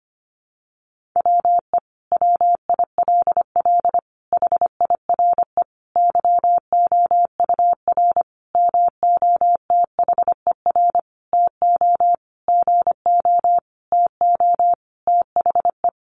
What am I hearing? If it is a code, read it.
Morse code, heard as a keyed tone, about 25 wpm, 700 Hz. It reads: WE WILL HIRE YOUR MOTHER TO GO TO THE
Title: Sentences from Top 1400 Words 25wpm